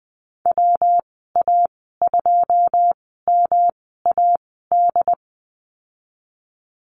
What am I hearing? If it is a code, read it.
Morse code, heard as a keyed tone, about 20 wpm, spaced as for 14 wpm, 700 Hz. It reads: WA2MAD
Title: 140 Lesson 23g - Letter f - Call Signs 20wpm